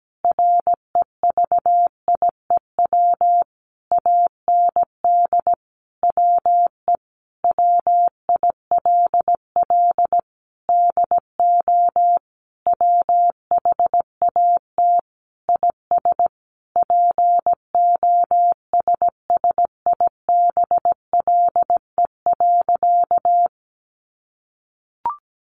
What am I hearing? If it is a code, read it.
Morse code, heard as a keyed tone, about 17 wpm, 700 Hz. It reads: REVIEW AND WE WILL DO WHAT IS POSSIBLE.
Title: Sentences from Top 900 Words 17wpm